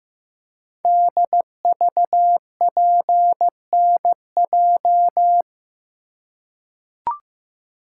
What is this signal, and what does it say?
Morse code, heard as a keyed tone, about 15 wpm, 700 Hz. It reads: DVPNJ
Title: Code Groups - 5 Letters - 15wpm